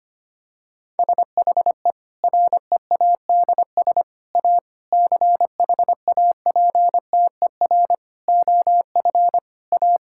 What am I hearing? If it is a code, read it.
Morse code, heard as a keyed tone, about 25 wpm, 700 Hz. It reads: SHE READS A CHAPTER OF A
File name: practice / 25wpm / Difficult Sentences with Hints - 25wpm